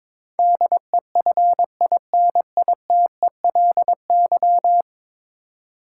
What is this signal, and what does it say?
Morse code, heard as a keyed tone, about 22 wpm, 700 Hz. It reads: DEFINITELY
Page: morse-code-ninja-practice-22wpm